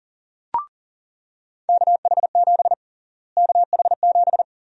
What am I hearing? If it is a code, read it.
Morse code, heard as a keyed tone, about 41 wpm, 700 Hz. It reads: KH7 KH7
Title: USA Call Sign Prefixes - Repeated 2x - 40wpm